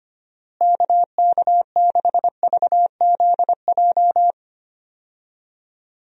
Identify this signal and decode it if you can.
Morse code, heard as a keyed tone, about 25 wpm, 700 Hz. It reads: KK6VZJ
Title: USA Call Signs - 6 Characters - Repeated 2x - 25wpm